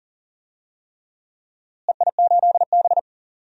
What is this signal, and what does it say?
Morse code, heard as a keyed tone, about 40 wpm, 700 Hz. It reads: EI8B
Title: DXCC-100 Call Signs - 4 Characters - Repeated 3x - 40wpm